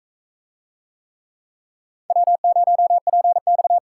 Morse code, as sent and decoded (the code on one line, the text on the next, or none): .-- ----- .--. -..-
W0PX